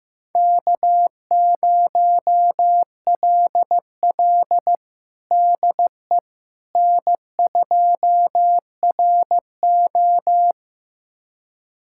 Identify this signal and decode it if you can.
Morse code, heard as a keyed tone, about 15 wpm, 700 Hz. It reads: K0LL DE N2RO
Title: USA Callsign DE Callsign - 4 Characters - 15wpm